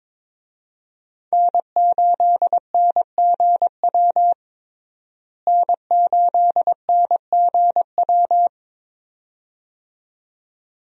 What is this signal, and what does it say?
Morse code, heard as a keyed tone, about 22 wpm, 700 Hz. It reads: N8NGW N8NGW
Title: USA Call Signs - 5 Characters - Repeated 2x - 22wpm